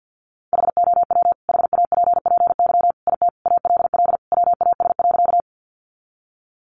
Morse code, heard as a keyed tone, about 48 wpm, 700 Hz. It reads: HOW HAPPY IT ALL WAS.
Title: Sentences from Top 1000 Words 50wpm